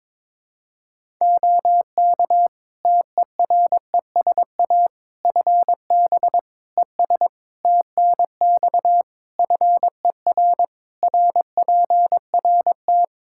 Morse code, heard as a keyed tone, about 22 wpm, 700 Hz. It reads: OK TERESA FB ES TNX FER RPRT
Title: Practice QSO - Exchange 2 - Info - 22wpm